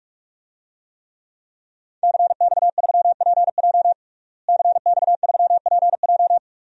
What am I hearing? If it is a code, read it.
Morse code, heard as a keyed tone, about 41 wpm, 700 Hz. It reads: CX3PJ CX3PJ
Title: DXCC-100 Call Signs - 5 Characters - Repeated 3x - 45wpm